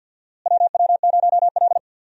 Morse code, heard as a keyed tone, about 53 wpm, 700 Hz. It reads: WW0L
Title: USA Call Signs - 4 Characters - Repeated 2x - 50wpm